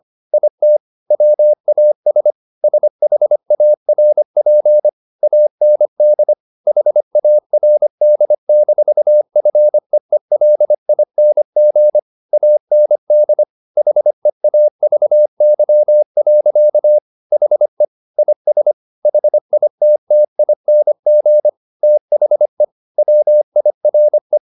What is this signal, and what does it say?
Morse code, heard as a keyed tone, about 25 wpm, 600 Hz. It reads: IT WAS SHARP AND HARD-FEELING AND HEAVY. HE IS HITTING THE WIRE